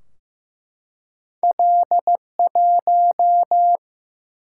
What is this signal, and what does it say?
Morse code, heard as a keyed tone, about 15 wpm, 700 Hz. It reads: L1
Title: Single-Letter Single-Number 15wpm